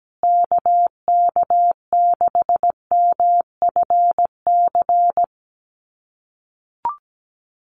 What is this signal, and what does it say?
Morse code, heard as a keyed tone, about 17 wpm, 700 Hz. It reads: KK6MFC
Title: USA Call Signs Repeated 2x 17wpm